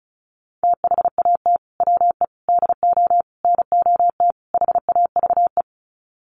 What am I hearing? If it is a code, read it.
Morse code, heard as a keyed tone, about 34 wpm, 700 Hz. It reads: THAT WE DO NOT HAVE
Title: Sets of 5 Words Encore 35wpm